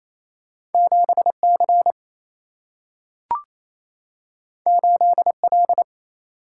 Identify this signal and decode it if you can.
Morse code, heard as a keyed tone, about 28 wpm, 700 Hz. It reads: 7C 8L
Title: Single-Letter Single-Number 28wpm